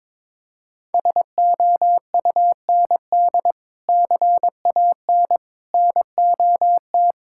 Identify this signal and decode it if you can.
Morse code, heard as a keyed tone, about 22 wpm, 700 Hz. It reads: SOUND CAN NOT
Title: Sentences from Top 1000 Words - Review Part 6 - 22wpm